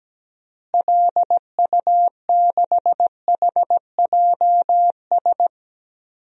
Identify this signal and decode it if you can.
Morse code, heard as a keyed tone, about 17 wpm, 700 Hz. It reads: LU6HJS
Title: DXCC-100 Call Signs - 6 Characters - Repeated 3x - 17wpm